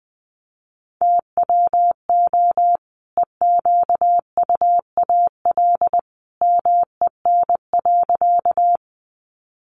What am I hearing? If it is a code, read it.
Morse code, heard as a keyed tone, about 20 wpm, 700 Hz. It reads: TWO EQUAL MEN.
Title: Sentences from Top 1400 Words 20wpm